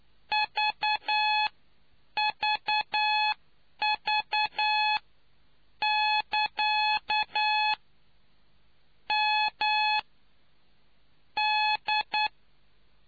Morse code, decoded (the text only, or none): VVV<KA>MD